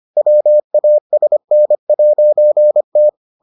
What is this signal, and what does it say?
Morse code, heard as a keyed tone, about 25 wpm, 600 Hz. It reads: WASN'T